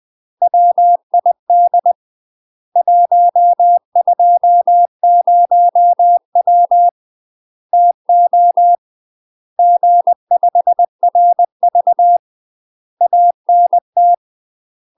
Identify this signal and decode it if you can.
Morse code, heard as a keyed tone, about 20 wpm, 700 Hz. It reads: WID 120W TO G5RV ANT
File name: 20WPM